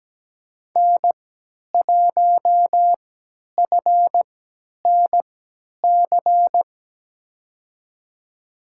Morse code, heard as a keyed tone, about 17 wpm, spaced as for 10 wpm, 700 Hz. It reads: N1FNC
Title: USA Call Signs - 5 Characters - 3x Character Spacing - 17wpm